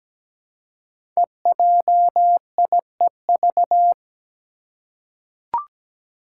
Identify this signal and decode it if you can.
Morse code, heard as a keyed tone, about 17 wpm, 700 Hz. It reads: EJIEV
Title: Five Challenging Letter-Number Combinations 17wpm